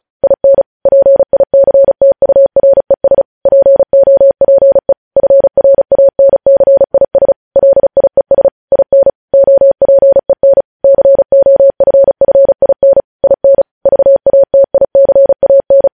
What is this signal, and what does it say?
Morse code, heard as a keyed tone, about 36 wpm, 550 Hz. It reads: IN PICTURES POPE FRANCIS LIES IN OPEN COFFIN IN VATICAN
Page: news headlines in morse code at 35 wpm